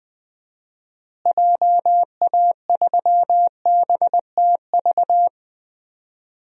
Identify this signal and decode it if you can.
Morse code, heard as a keyed tone, about 21 wpm, 700 Hz. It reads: JA3BTV
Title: DXCC-100 Call Signs - 6 Characters - Repeated 3x - 20wpm